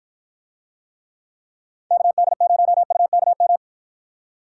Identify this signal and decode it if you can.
Morse code, heard as a keyed tone, about 52 wpm, 700 Hz. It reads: KD0UKM